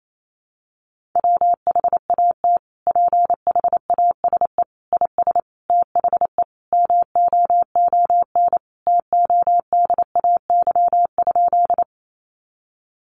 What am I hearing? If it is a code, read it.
Morse code, heard as a keyed tone, about 28 wpm, 700 Hz. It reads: WHAT PHASE IS THE MOON TODAY?
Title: Sentences from Top 1200 Words 28wpm